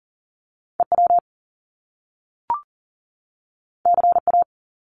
Morse code, heard as a keyed tone, about 38 wpm, 700 Hz. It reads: EW CA
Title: Top Two Letter Combinations in Words 40wpm